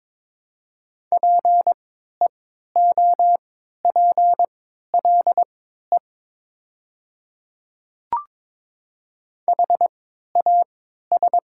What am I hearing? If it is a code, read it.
Morse code, heard as a keyed tone, about 22 wpm, spaced as for 13 wpm, 700 Hz. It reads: PEOPLE HAS
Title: 3x Character Spacing - Top 100 Words - 22wpm